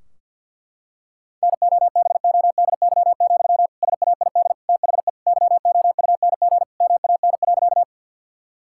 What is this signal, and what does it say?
Morse code, heard as a keyed tone, about 51 wpm, 700 Hz. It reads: NOBODY, SAID THE YOUNG MAN.
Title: Sentences from Top 1600 Words 50wpm